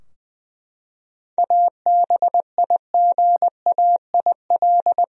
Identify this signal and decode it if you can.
Morse code, heard as a keyed tone, about 20 wpm, 700 Hz. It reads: ABIGAIL
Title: USA Common Names - 20wpm